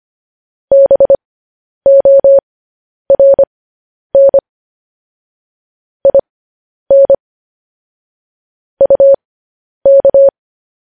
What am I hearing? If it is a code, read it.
Morse code, heard as a keyed tone, about 25 wpm, spaced as for 10 wpm, 550 Hz. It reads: BORN IN UK
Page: news headlines in morse code at 10 wpm